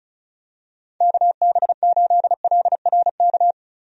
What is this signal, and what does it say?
Morse code, heard as a keyed tone, about 36 wpm, 700 Hz. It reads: KB8LRK
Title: USA Call Signs - 6 Characters - Repeated 3x - 35wpm